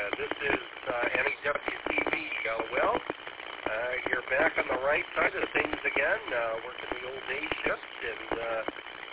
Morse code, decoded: none

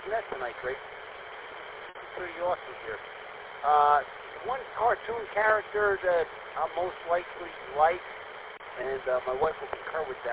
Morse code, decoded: none

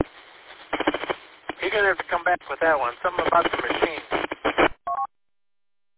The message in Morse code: none

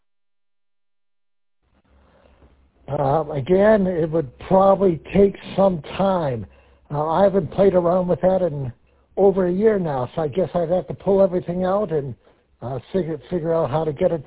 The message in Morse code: none

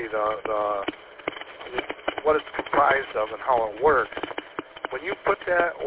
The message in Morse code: none

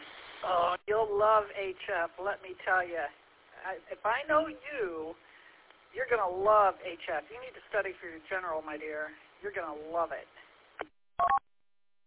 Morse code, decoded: none

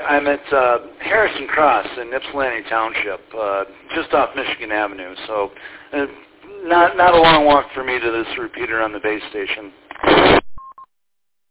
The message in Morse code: none

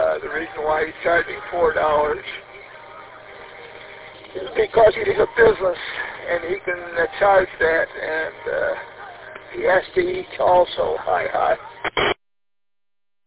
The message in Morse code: none

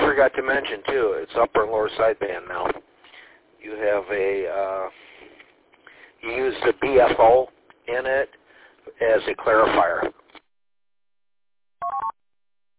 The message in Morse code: none

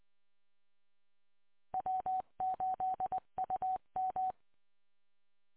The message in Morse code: W8UM

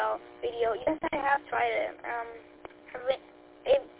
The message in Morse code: none

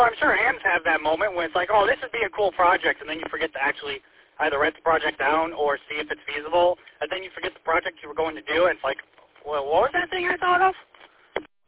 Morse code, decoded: none

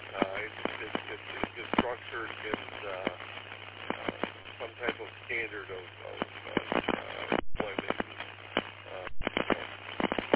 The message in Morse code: none

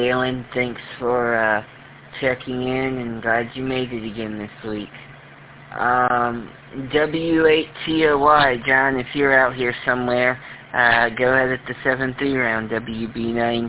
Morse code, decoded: none

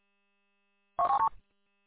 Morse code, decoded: none

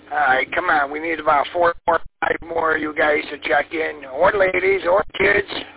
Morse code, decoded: none

